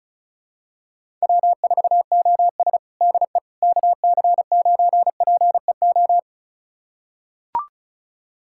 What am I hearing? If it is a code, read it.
Morse code, heard as a keyed tone, about 34 wpm, 700 Hz. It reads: W4OS DE KC9PEO
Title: USA Callsign DE Callsign 35wpm